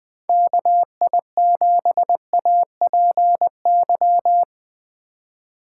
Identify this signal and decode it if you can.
Morse code, heard as a keyed tone, about 20 wpm, 700 Hz. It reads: KI7APY
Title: USA Call Signs - 6 Characters - 3x Character Spacing - 20wpm